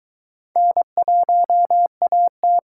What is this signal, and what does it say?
Morse code, heard as a keyed tone, about 22 wpm, 700 Hz. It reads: N1AT